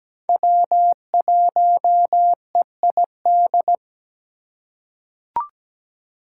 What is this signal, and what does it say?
Morse code, heard as a keyed tone, about 17 wpm, 700 Hz. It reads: W1EID
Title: USA Call Signs - 5 Characters - 4x Character Spacing - 17wpm